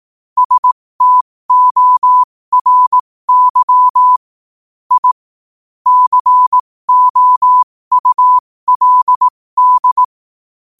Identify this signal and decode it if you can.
Morse code, heard as a keyed tone, about 18 wpm, spaced as for 15 wpm, 1000 Hz. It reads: STORY I COULD
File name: qotdfiles / 15wpm